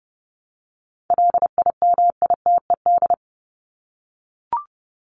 Morse code, heard as a keyed tone, about 30 wpm, 700 Hz. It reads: LIMITED
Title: Top 751-1000 Words 30wpm